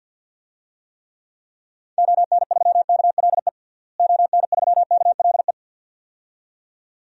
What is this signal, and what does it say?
Morse code, heard as a keyed tone, about 53 wpm, 700 Hz. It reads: ON3KLE ON3KLE
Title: DXCC-100 Call Signs - 6 Characters - Repeated 2x - 50wpm